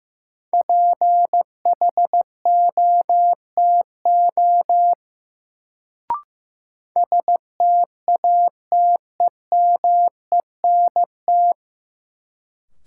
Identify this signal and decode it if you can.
Morse code, heard as a keyed tone, about 15 wpm, 700 Hz. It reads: PHOTO STATEMENT
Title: Sentences from Top 1200 Words - New Word Drill Alt - 15wpm